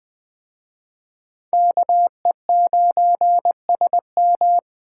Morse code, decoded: KE9SM

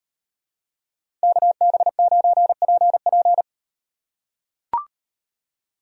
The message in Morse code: KB9PP

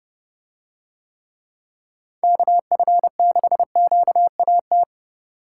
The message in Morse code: KF6QAT

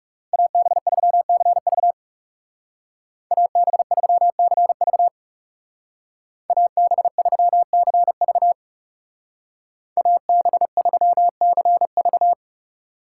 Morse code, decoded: AB3CV AB3CV AB3CV AB3CV

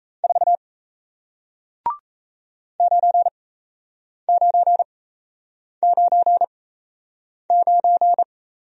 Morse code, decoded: 4 9999

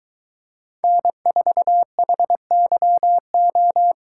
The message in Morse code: N4HYO

